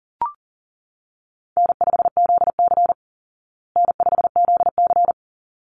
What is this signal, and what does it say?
Morse code, heard as a keyed tone, about 40 wpm, 700 Hz. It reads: N5ZC N5ZC
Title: Easy Call Signs Repeated 2x - 40wpm